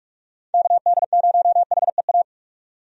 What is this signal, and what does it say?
Morse code, heard as a keyed tone, about 46 wpm, 700 Hz. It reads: KD0HEA
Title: USA Call Signs - 6 Characters - 45wpm